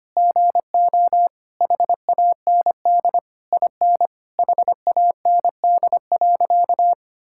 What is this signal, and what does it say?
Morse code, heard as a keyed tone, about 25 wpm, 700 Hz. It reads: GO HAND IN HAND.
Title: Sentences from Top 1000 Words - Review Part 5 - 25wpm